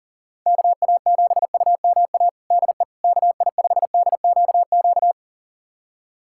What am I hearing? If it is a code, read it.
Morse code, heard as a keyed tone, about 40 wpm, 700 Hz. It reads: KA7UMA DE KI5DQQ
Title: USA Callsign DE Callsign - 6 Characters - 40wpm